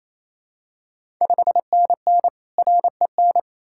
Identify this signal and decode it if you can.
Morse code, heard as a keyed tone, about 28 wpm, 700 Hz. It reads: 5NN REN
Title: Ontario QSO Party Exchange 28wpm